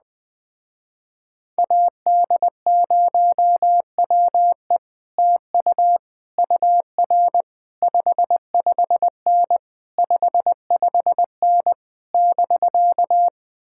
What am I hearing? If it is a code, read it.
Morse code, heard as a keyed tone, about 20 wpm, 700 Hz. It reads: AD0WE TU UR 55N 55N <BK>
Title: Speed-Racing Common QSO Phrases - 17wpm